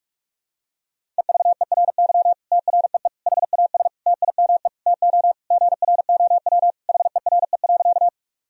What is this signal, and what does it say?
Morse code, heard as a keyed tone, about 42 wpm, 700 Hz. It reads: EVERY TREE HAS TIME TO GROW HERE.
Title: Sentences from Top 1000 Words - Review Part 1 - 45wpm